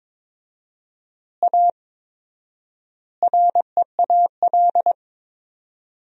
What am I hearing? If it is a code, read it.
Morse code, heard as a keyed tone, about 22 wpm, 700 Hz. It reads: A REAL
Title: 4x Word-Spacing - Sentences from Top 500 Words - 22wpm